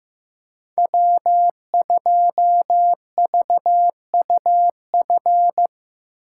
Morse code, heard as a keyed tone, about 15 wpm, 700 Hz. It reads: W2VUF
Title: K1USN SST QSO Practice - 15wpm